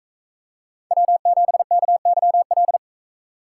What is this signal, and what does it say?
Morse code, heard as a keyed tone, about 42 wpm, 700 Hz. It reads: W7KYL